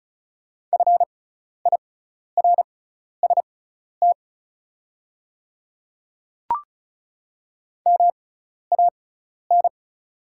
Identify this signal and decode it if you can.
Morse code, heard as a keyed tone, about 36 wpm, spaced as for 12 wpm, 690 Hz. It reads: FIRST MAN
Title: 6x Character Spacing - Top 300 Words - 35wpm